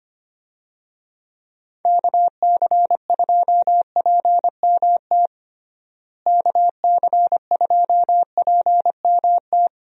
Morse code, heard as a keyed tone, about 25 wpm, 700 Hz. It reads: KC2PMT KC2PMT
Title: USA Call Signs - 6 Characters - Repeated 2x - 25wpm